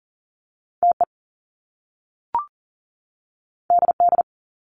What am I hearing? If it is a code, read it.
Morse code, heard as a keyed tone, about 41 wpm, 700 Hz. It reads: TE DD
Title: Top Two Letter Combinations in Words 40wpm